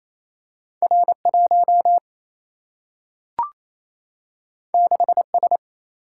Morse code, heard as a keyed tone, about 28 wpm, 700 Hz. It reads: R 1 6 S
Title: Single-Letter Single-Number 28wpm